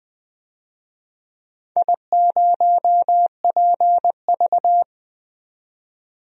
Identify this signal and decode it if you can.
Morse code, heard as a keyed tone, about 20 wpm, 700 Hz. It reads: I0PV
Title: DXCC-100 Call Signs - 4 Characters - 20wpm